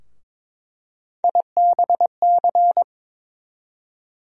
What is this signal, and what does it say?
Morse code, heard as a keyed tone, about 22 wpm, 700 Hz. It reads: IBC
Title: USA Call Sign Postfixes - 3 Letters 22wpm